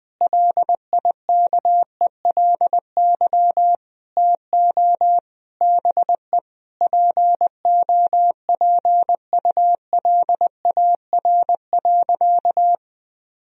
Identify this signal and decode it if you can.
Morse code, heard as a keyed tone, about 20 wpm, 700 Hz. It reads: LIKELY TO BE POPULAR.
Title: Sentences from Top 1000 Words - Review Part 4 - 20wpm